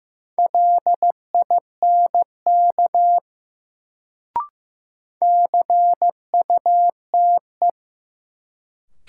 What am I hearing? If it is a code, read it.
Morse code, heard as a keyed tone, about 16 wpm, 700 Hz. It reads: LINK CUTE
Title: Sentences from Top 1000 Words - Review Part 7 - New Word Drill - 15wpm